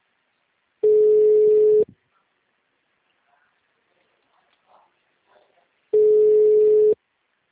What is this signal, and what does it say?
Ringback tone (Europe, 425 Hz).
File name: out- -2013- - - .18778